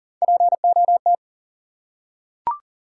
Morse code, heard as a keyed tone, about 42 wpm, 700 Hz. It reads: POT
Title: Three Letter Words - 40wpm